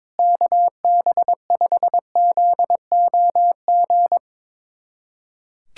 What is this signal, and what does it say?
Morse code, heard as a keyed tone, about 22 wpm, 700 Hz. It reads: KB5ZOG
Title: USA Callsign DE Callsign - 6 Characters - 22wpm